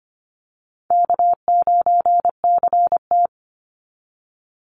Morse code, heard as a keyed tone, about 25 wpm, 700 Hz. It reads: K9CT
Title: Easy Call Signs Repeated 2x - 25wpm